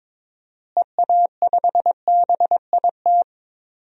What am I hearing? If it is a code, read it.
Morse code, heard as a keyed tone, about 22 wpm, 700 Hz. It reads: EA5BIT
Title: DXCC-100 Call Signs - 6 Characters - Repeated 2x - 22wpm